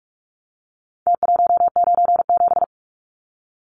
Morse code, heard as a keyed tone, about 44 wpm, 700 Hz. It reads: T197
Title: Numbers - Contest Style - 4-Digit 45wpm